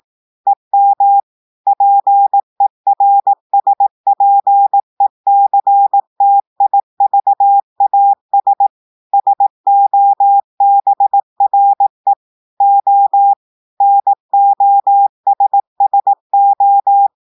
Morse code, heard as a keyed tone, about 18 wpm, 800 Hz. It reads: EM PERSPECTIVAS SOBRE O NOSSO
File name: audios / portugais / portugais texte en clair 1 vitesse 18 mots minutes